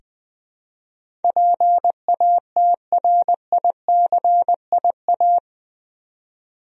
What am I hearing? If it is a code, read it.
Morse code, heard as a keyed tone, about 20 wpm, 700 Hz. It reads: PATRICIA
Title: USA Common Names - 20wpm